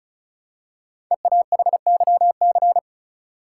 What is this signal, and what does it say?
Morse code, heard as a keyed tone, about 36 wpm, 700 Hz. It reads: EAHYC
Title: Code Groups - 5 Letters - 35wpm